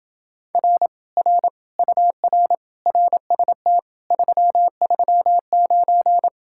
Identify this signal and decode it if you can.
Morse code, heard as a keyed tone, about 28 wpm, 700 Hz. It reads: R R UR RST 339